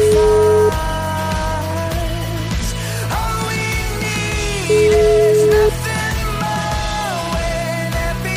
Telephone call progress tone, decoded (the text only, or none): ringback tone